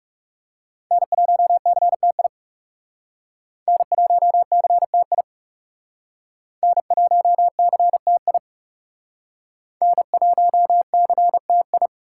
Morse code, heard as a keyed tone, about 39 wpm, 700 Hz. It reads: N1CTI N1CTI N1CTI N1CTI